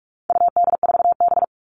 Morse code, heard as a keyed tone, about 44 wpm, 700 Hz. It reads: UD4B
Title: All Cut Numbers - 4-Digit 45wpm